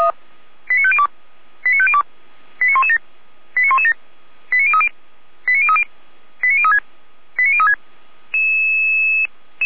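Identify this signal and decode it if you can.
Five-tone selective call (ZVEI), heard as quick runs of five tones, tones of about 70 ms. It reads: 89502 89502 89117 89117 89030 89030 89036 89036
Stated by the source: Five-tone ZVEI selective call received on an SDR.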